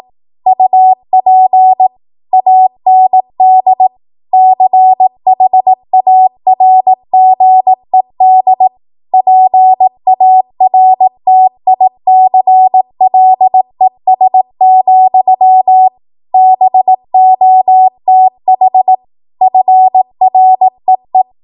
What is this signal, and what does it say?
Morse code, heard as a keyed tone, about 18 wpm, 750 Hz. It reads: UP AND CHARGED PARTICLES, BOTH FREE